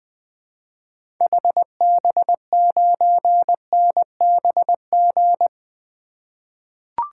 Morse code, heard as a keyed tone, about 20 wpm, 700 Hz. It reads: HB9NBG
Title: 2023 SOTA Activator Call Signs - 20wpm